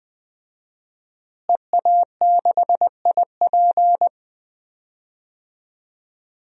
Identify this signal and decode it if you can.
Morse code, heard as a keyed tone, about 20 wpm, 700 Hz. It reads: EA6IP
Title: DXCC-100 Call Signs - 5 Characters - Repeated 3x - 20wpm